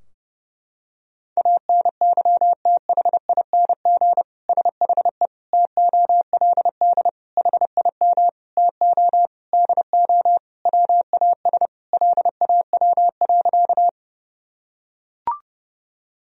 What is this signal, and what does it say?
Morse code, heard as a keyed tone, about 32 wpm, 700 Hz. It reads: ANYTHING SHE TOLD HIM TO DO WAS LAW.
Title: Sentences from Top 1000 Words - Review Part 3 - 30wpm